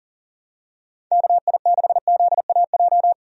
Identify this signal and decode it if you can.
Morse code, heard as a keyed tone, about 38 wpm, 700 Hz. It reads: KI6ZAJ
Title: USA Call Signs - 6 Characters - 40wpm